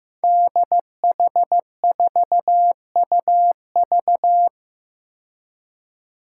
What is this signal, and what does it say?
Morse code, heard as a keyed tone, about 15 wpm, 700 Hz. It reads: DH4UV
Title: Five Challenging Letter-Number Combinations 15wpm